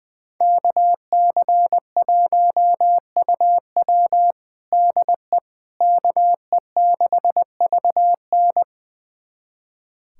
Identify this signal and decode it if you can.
Morse code, heard as a keyed tone, about 20 wpm, 700 Hz. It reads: KC1UW DE KE6VN
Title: USA Callsign DE Callsign - 5 Characters - 20wpm